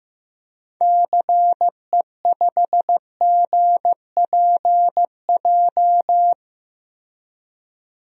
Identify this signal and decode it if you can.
Morse code, heard as a keyed tone, about 15 wpm, 700 Hz. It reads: CE5GPJ
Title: DXCC-100 Call Signs - 6 Characters - 15wpm